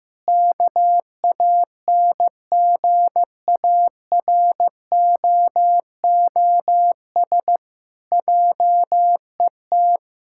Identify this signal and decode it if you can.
Morse code, heard as a keyed tone, about 16 wpm, 700 Hz. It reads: KANGAROOS JET
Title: Pangram Challenge v2 - 15wpm